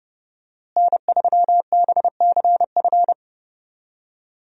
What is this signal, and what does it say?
Morse code, heard as a keyed tone, about 30 wpm, 700 Hz. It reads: N3BCF
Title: USA Call Signs - 5 Characters - Repeated 3x - 30wpm